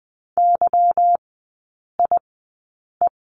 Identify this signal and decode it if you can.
Morse code, heard as a keyed tone, about 20 wpm, spaced as for 8 wpm, 700 Hz. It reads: YIE
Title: 152 Lesson 25e - Letter p - Three Characters 20wpm